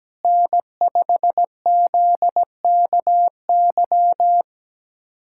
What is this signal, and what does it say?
Morse code, heard as a keyed tone, about 17 wpm, 700 Hz. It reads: N5ZKY
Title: USA Callsign DE Callsign 17wpm